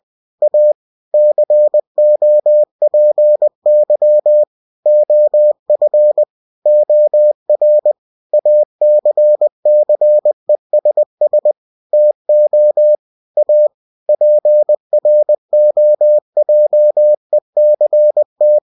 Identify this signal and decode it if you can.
Morse code, heard as a keyed tone, about 20 wpm, 600 Hz. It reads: A COPY OF OR ACCESS TO A PROJECT
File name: cw-books / winnie-the-pooh / winnie-20wpm-page-0113